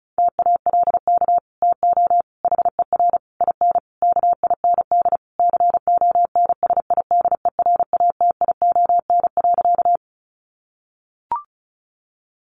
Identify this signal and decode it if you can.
Morse code, heard as a keyed tone, about 34 wpm, 700 Hz. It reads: TALK TO HER IN KIND CONSIDERATION.
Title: Sentences from Top 1500 Words 35wpm